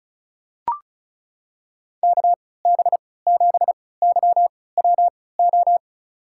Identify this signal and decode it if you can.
Morse code, heard as a keyed tone, about 36 wpm, spaced as for 20 wpm, 700 Hz. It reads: KB7YWO
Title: USA Call Signs - 6 Characters - 3x Character Spacing - 35wpm